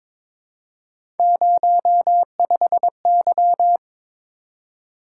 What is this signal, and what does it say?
Morse code, heard as a keyed tone, about 23 wpm, 700 Hz. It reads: 05Y
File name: practice / 22wpm / Three Letter-Number Combinations - 22wpm